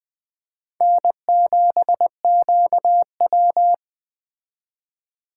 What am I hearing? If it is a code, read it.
Morse code, heard as a keyed tone, about 20 wpm, 700 Hz. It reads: N7QW
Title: Speed Racing - USA 1x2 Call Signs 17wpm